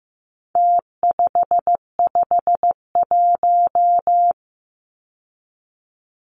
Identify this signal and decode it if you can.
Morse code, heard as a keyed tone, about 15 wpm, 700 Hz. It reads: T551
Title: Numbers - Contest Style - 4-Digit 15wpm